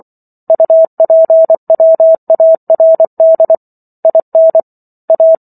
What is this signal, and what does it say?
Morse code, heard as a keyed tone, about 24 wpm, 650 Hz. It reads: UPWARD IN A
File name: Chapter 15 - 24 WPM